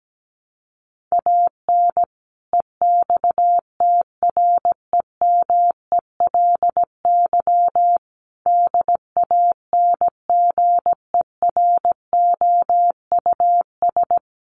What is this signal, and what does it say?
Morse code, heard as a keyed tone, about 17 wpm, 700 Hz. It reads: AN EXTREMELY DANGEROUS
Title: Sentences from Top 1200 Words 17wpm